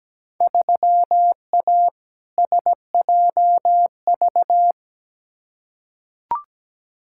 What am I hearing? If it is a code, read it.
Morse code, heard as a keyed tone, about 17 wpm, 700 Hz. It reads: 3A SJV
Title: Field Day Exchanges Encore - 17wpm